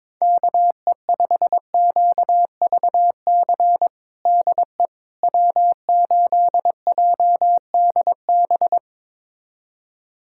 Morse code, heard as a keyed tone, about 21 wpm, 700 Hz. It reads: KE5QVC DE W8JDB
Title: USA Callsign DE Callsign 22wpm